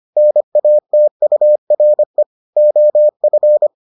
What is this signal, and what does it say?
Morse code, heard as a keyed tone, about 25 wpm, 600 Hz. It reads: NATURE OF